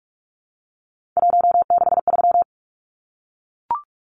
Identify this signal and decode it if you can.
Morse code, heard as a keyed tone, about 44 wpm, 700 Hz. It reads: 163
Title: Numbers - 3-Digit 45wpm